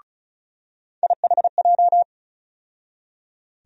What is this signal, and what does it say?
Morse code, heard as a keyed tone, about 36 wpm, 700 Hz. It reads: IHJ
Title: USA Call Sign Postfixes - 3 Letters 35wpm